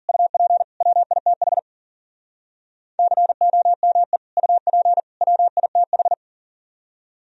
Morse code, heard as a keyed tone, about 54 wpm, 700 Hz. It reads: UP WITH COME UP WITH